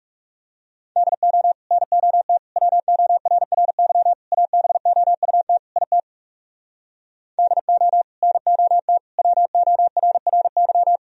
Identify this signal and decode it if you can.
Morse code, heard as a keyed tone, about 40 wpm, 700 Hz. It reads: DO NOT WORRY ABOUT IT DO NOT WORRY